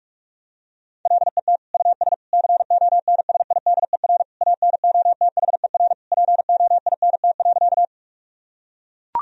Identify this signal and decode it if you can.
Morse code, heard as a keyed tone, about 43 wpm, 700 Hz. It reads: LET US CONSIDER ANOTHER POINT.
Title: Sentences from Top 1000 Words - Review Part 4 - 45wpm